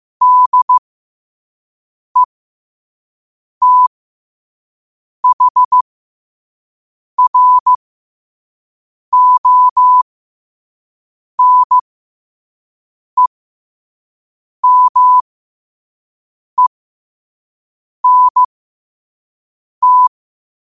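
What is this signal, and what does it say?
Morse code, heard as a keyed tone, about 15 wpm, spaced as for 5 wpm, 1000 Hz. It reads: DETHRONEMENT
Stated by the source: Morse sent at 5 wpm.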